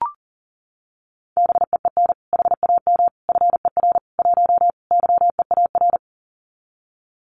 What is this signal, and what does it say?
Morse code, heard as a keyed tone, about 37 wpm, 700 Hz. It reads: BEEN HAM FER 1 YEAR
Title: Common QSO Phrases Encore 40wpm